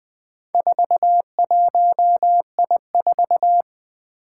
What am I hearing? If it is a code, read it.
Morse code, heard as a keyed tone, about 20 wpm, 700 Hz. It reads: 41I4
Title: Four Challenging Letter-Number Combinations 20wpm